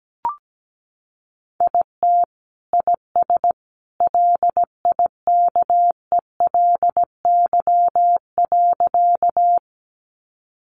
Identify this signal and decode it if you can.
Morse code, heard as a keyed tone, about 17 wpm, 700 Hz. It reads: IT IS LIKELY.
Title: Sentences from Top 600 Words 17wpm